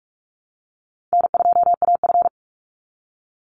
Morse code, heard as a keyed tone, about 44 wpm, 700 Hz. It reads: N2AF
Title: NA SOTA Chaser Callsigns 45wpm